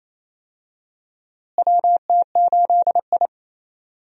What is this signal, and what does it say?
Morse code, heard as a keyed tone, about 28 wpm, 700 Hz. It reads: WT8I
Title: USA Call Signs - 4 Characters - Repeated 2x - 28wpm